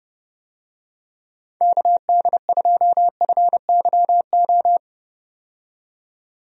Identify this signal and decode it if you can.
Morse code, heard as a keyed tone, about 31 wpm, 700 Hz. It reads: KD2FYO
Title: USA Call Signs - 6 Characters - 30wpm